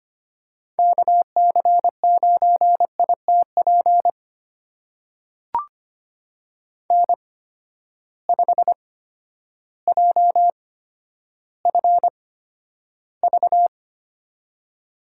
Morse code, heard as a keyed tone, about 25 wpm, spaced as for 11 wpm, 700 Hz. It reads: KC9ITP N5JFV
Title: USA Call Signs - 8x Character Spacing - 25wpm